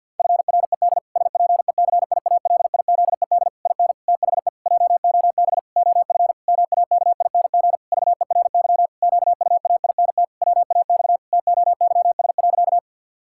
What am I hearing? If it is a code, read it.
Morse code, heard as a keyed tone, about 50 wpm, 700 Hz. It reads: FRED SPECIALIZED IN THE JOB OF MAKING VERY QUAINT WAX TOYS.
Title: Pangram Challenge v2 - 50wpm